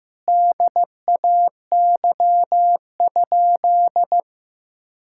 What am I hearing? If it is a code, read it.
Morse code, heard as a keyed tone, about 15 wpm, 700 Hz. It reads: DAY?
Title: Sentences from Top 1000 Words - Review Part 6 - 15wpm